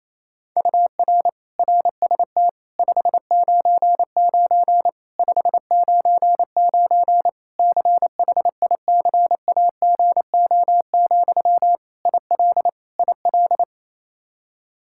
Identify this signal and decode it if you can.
Morse code, heard as a keyed tone, about 28 wpm, 700 Hz. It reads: UR RST 599 599 CHICAGO, IL IL
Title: Practice QSO - Exchange 1 - Beginning - 28wpm